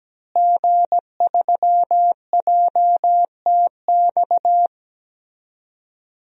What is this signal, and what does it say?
Morse code, heard as a keyed tone, about 17 wpm, 700 Hz. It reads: G3JTX
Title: DXCC-100 Call Signs - 5 Characters - Repeated 3x - 17wpm